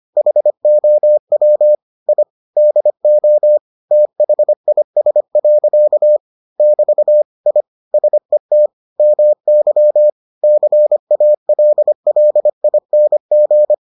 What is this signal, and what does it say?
Morse code, heard as a keyed tone, about 25 wpm, 600 Hz. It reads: HOW I DO THIS. <BT> I SET MY CALLING